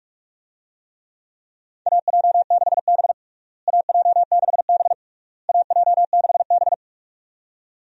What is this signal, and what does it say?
Morse code, heard as a keyed tone, about 46 wpm, 700 Hz. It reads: AJ6B AJ6B AJ6B
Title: USA Call Signs - 4 Characters - Repeated 3x - 45wpm